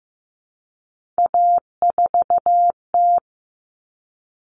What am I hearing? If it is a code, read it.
Morse code, heard as a keyed tone, about 15 wpm, 700 Hz. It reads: A4T
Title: All Cut Numbers - 3-Digit 15wpm